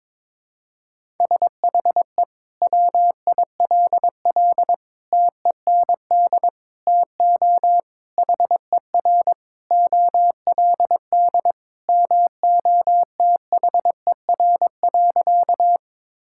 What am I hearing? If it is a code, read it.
Morse code, heard as a keyed tone, about 22 wpm, 700 Hz. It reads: SHE WILL TEND TO HER OLD MOTHER.
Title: Sentences from Top 1000 Words - Review Part 5 - 22wpm